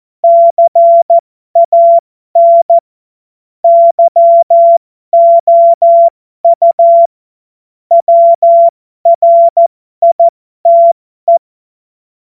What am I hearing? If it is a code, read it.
Morse code, heard as a keyed tone, about 14 wpm, spaced as for 12 wpm, 680 Hz. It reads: CAN YOU WRITE
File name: Morsekurs / Witze